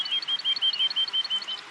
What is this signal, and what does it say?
There is no code.